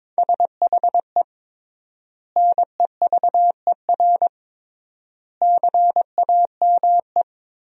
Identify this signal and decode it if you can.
Morse code, heard as a keyed tone, about 22 wpm, 700 Hz. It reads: SHE NEVER CAME
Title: 3x Word-Spacing - Sentences from Top 300 Words - 22wpm